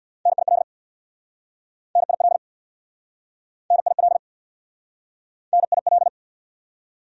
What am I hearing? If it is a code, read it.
Morse code, heard as a keyed tone, about 69 wpm, 700 Hz. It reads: NIL NIL NIL NIL